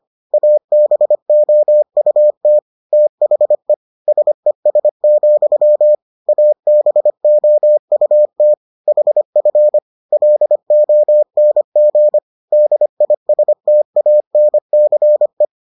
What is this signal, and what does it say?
Morse code, heard as a keyed tone, about 25 wpm, 600 Hz. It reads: ABOUT THE SES, ABOUT HF LONG DISTANCE